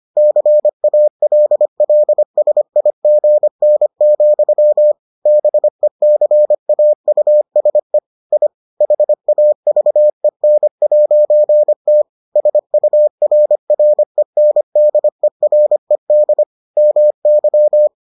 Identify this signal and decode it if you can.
Morse code, heard as a keyed tone, about 25 wpm, 600 Hz. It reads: CALLSIGN, BECAUSE I HAVEN'T SURRENDERED MY